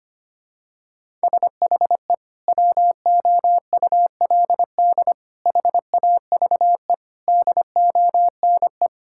Morse code, heard as a keyed tone, about 25 wpm, 700 Hz. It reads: SHE WOULD HAVE DONE
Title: Sentences from Top 1600 Words 25wpm